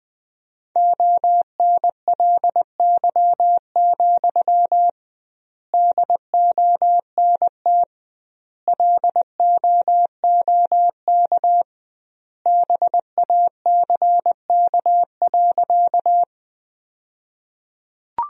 Morse code, heard as a keyed tone, about 20 wpm, 700 Hz. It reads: ONLY, DONT LOOK BACK.
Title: 2x Word-Spacing - Sentences from Top 300 Words - 20wpm